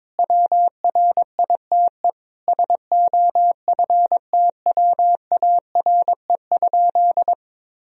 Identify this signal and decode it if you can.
Morse code, heard as a keyed tone, about 22 wpm, 700 Hz. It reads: WRITE SOFTWARE?
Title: Sentences from Top 1000 Words - Review Part 3 - 22wpm